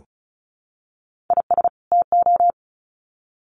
Morse code, heard as a keyed tone, about 34 wpm, 700 Hz. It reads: IS TO
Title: Sets of 2 Words 35wpm